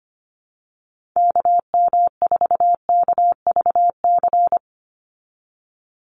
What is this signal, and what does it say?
Morse code, heard as a keyed tone, about 25 wpm, 700 Hz. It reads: KM4KVC
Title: USA Call Signs 25wpm